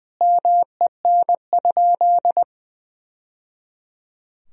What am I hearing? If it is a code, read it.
Morse code, heard as a keyed tone, about 20 wpm, 700 Hz. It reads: MEN?
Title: Sentences from Top 1000 Words - Review Part 1 - 20wpm